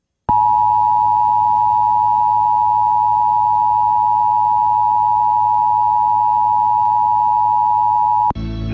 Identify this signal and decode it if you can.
Emergency Alert System (SAME) tones (the 853 and 960 Hz attention tone). It is heard from 0.3 to 8.4 s.